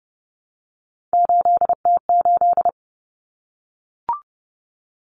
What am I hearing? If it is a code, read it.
Morse code, heard as a keyed tone, about 30 wpm, 700 Hz. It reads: 8T8
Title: Common Cut Numbers - 3-Digit 30wpm